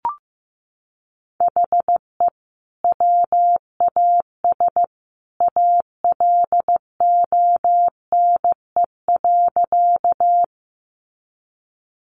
Morse code, heard as a keyed tone, about 15 wpm, 700 Hz. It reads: HE WAS ALONE.
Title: Sentences from Top 1000 Words 15wpm